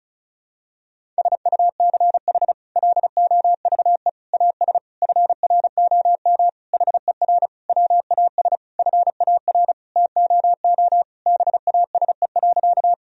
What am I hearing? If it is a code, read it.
Morse code, heard as a keyed tone, about 36 wpm, 700 Hz. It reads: SUCH LOVE AS FROM HER WAS FAR TOO BASE.
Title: Sentences from Top 1000 Words - Review Part 1 - 35wpm